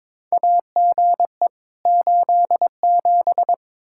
Morse code, heard as a keyed tone, about 22 wpm, 700 Hz. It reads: AGE 87
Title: K1USN SST QSO Practice 22wpm